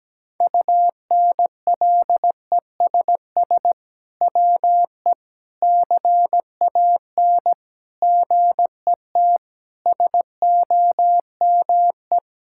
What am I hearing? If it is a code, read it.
Morse code, heard as a keyed tone, about 17 wpm, 700 Hz. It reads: UNLESS WE CAN GET SOME
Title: Sentences from Top 1000 Words - Review Part 6 - 17wpm